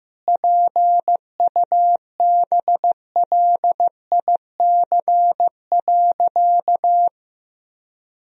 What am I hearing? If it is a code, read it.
Morse code, heard as a keyed tone, about 15 wpm, 700 Hz. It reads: PUBLIC.
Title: Sentences from Top 1000 Words - Review Part 3 - 15wpm